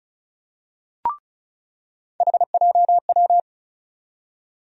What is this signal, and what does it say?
Morse code, heard as a keyed tone, about 36 wpm, 700 Hz. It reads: HJW
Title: USA Call Sign Postfixes - 3 Letters 35wpm